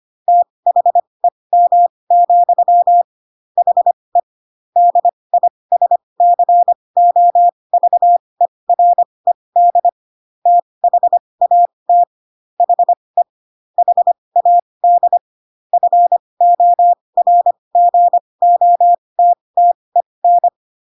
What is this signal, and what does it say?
Morse code, heard as a keyed tone, about 25 wpm, spaced as for 20 wpm, 700 Hz. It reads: THEM, HE DISCOVERED THAT HE HAD FORGOTTEN